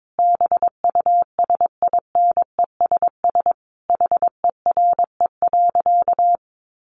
Morse code, heard as a keyed tone, about 22 wpm, 700 Hz. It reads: BUSINESS HERE.
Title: Sentences from Top 1300 Words 22wpm